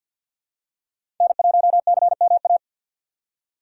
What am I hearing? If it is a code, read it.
Morse code, heard as a keyed tone, about 50 wpm, 700 Hz. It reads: N1CMA
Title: USA Call Signs - 5 Characters - 4x Character Spacing - 50wpm